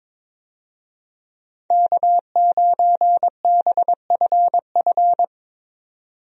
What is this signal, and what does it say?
Morse code, heard as a keyed tone, about 22 wpm, 700 Hz. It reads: K9BFF
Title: USA Call Signs - 5 Characters - Repeated 2x - 22wpm